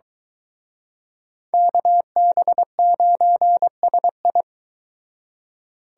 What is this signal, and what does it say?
Morse code, heard as a keyed tone, about 23 wpm, 700 Hz. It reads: KB9SI